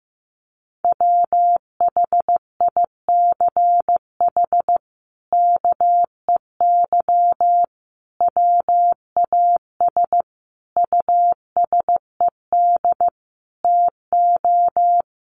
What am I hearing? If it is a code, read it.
Morse code, heard as a keyed tone, about 15 wpm, 700 Hz. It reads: WHICH KEY WAS USED TO
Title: Sentences from Top 600 Words 15wpm